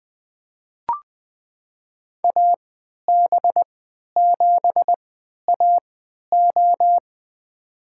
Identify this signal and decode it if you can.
Morse code, heard as a keyed tone, about 20 wpm, spaced as for 11 wpm, 700 Hz. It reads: AB7AO
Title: USA Call Signs - 5 Characters - 3x Character Spacing - 20wpm